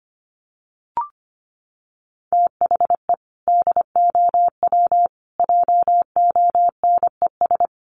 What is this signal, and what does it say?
Morse code, heard as a keyed tone, about 25 wpm, 700 Hz. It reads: THE DOW JONES
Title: Sets of 5 Words 25wpm